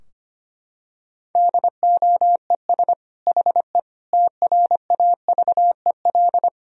Morse code, heard as a keyed tone, about 25 wpm, 700 Hz. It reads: DOES HE TRAVEL
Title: Sentences from Top 1000 Words - Review Part 2 - 25wpm